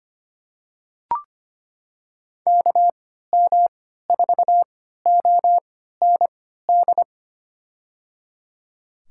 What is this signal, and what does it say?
Morse code, heard as a keyed tone, about 25 wpm, spaced as for 14 wpm, 700 Hz. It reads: KM4OND
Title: USA Call Signs - 6 Characters - 3x Character Spacing - 25wpm